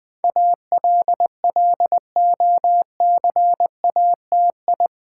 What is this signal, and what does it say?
Morse code, heard as a keyed tone, about 20 wpm, 700 Hz. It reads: ALLOCATI
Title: Build A Word - 4 Letters At A Time - 10-Letter Words - 20wpm